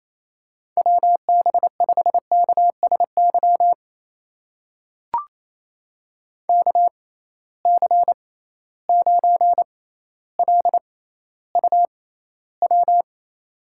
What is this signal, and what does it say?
Morse code, heard as a keyed tone, about 28 wpm, spaced as for 14 wpm, 700 Hz. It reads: WB5KSY KC9LUW
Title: USA Call Signs - 6 Characters - 6x Character Spacing - 28wpm